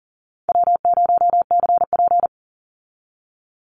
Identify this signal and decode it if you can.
Morse code, heard as a keyed tone, about 40 wpm, 700 Hz. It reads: W0CP
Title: Easy Call Signs - 40wpm